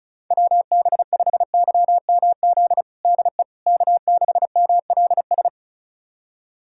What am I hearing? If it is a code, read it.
Morse code, heard as a keyed tone, about 36 wpm, 700 Hz. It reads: WB5YMZ DE K6MLS
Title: USA Callsign DE Callsign 35wpm